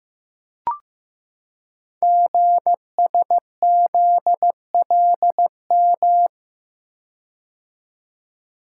Morse code, heard as a keyed tone, about 15 wpm, 690 Hz. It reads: GSZLM
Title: Code Groups - 5 Letters - 15wpm